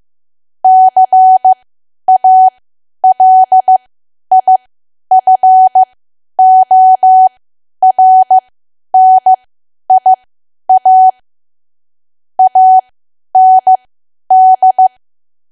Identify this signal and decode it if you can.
Morse code, heard as a keyed tone, about 15 wpm, spaced as for 10 wpm, 750 Hz. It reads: CALIFORNIA AND
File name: Archive / 10 WPM / 10WPM